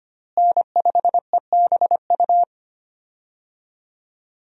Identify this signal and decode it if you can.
Morse code, heard as a keyed tone, about 25 wpm, 700 Hz. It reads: N5EBU
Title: USA Call Signs - 5 Characters - Repeated 3x - 25wpm